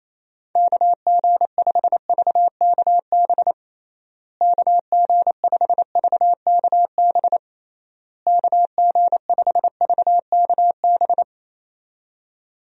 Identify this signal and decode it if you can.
Morse code, heard as a keyed tone, about 28 wpm, 700 Hz. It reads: KG5VKB KG5VKB KG5VKB
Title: USA Call Signs - 6 Characters - Repeated 3x - 28wpm